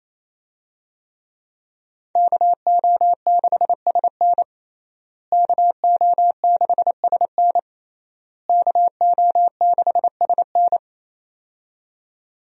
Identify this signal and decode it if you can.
Morse code, heard as a keyed tone, about 28 wpm, 700 Hz. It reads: KO6SN KO6SN KO6SN
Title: USA Call Signs - 5 Characters - Repeated 3x - 28wpm